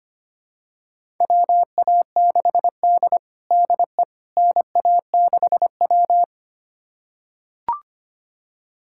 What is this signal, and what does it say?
Morse code, heard as a keyed tone, about 25 wpm, 700 Hz. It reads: WA6D DE NA6W
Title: USA Callsign DE Callsign - 4 Characters - 25wpm